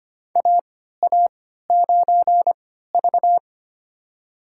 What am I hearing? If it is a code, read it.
Morse code, heard as a keyed tone, about 25 wpm, spaced as for 14 wpm, 700 Hz. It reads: AA9V
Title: USA Call Signs - 4 Characters - 3x Character Spacing - 25wpm